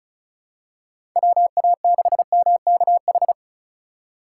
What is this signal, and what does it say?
Morse code, heard as a keyed tone, about 36 wpm, 700 Hz. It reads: WA6MKH
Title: USA Call Signs - 6 Characters - 2x Character Spacing - 35wpm